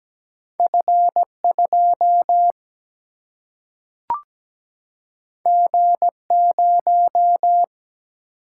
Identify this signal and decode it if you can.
Morse code, heard as a keyed tone, about 17 wpm, 700 Hz. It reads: F2 G0
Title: Single-Letter Single-Number 17wpm